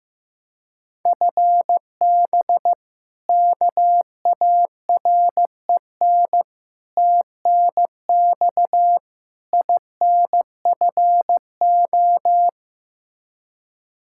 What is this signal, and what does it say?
Morse code, heard as a keyed tone, about 15 wpm, 700 Hz. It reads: FB KAREN TNX INFO
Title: K1USN SST QSO Practice 15wpm